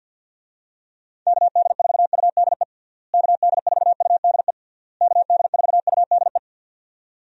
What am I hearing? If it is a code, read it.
Morse code, heard as a keyed tone, about 52 wpm, 700 Hz. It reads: KD4UDE KD4UDE KD4UDE
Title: USA Call Signs - 6 Characters - Repeated 3x - 50wpm